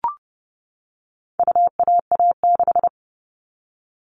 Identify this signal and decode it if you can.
Morse code, heard as a keyed tone, about 30 wpm, 700 Hz. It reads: UAA6
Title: All Cut Numbers - 4-Digit 30wpm